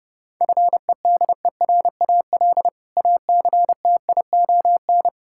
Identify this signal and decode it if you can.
Morse code, heard as a keyed tone, about 32 wpm, 700 Hz. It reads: FEDERAL ACTION
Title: Sentences from Top 1000 Words - Review Part 5 - 30wpm